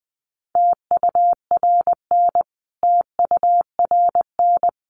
Morse code, heard as a keyed tone, about 20 wpm, 700 Hz. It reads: TURN TURN
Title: Top 500 Words Repeated 20wpm